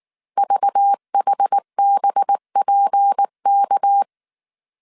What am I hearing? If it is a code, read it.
Morse code, heard as a keyed tone, about 19 wpm, 800 Hz. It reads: VHBPK